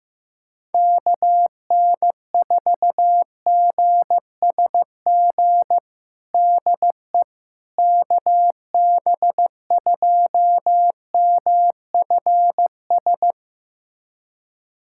Morse code, heard as a keyed tone, about 15 wpm, 700 Hz. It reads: KN4GSG DE KB2MFS
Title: USA Callsign DE Callsign - 6 Characters - 15wpm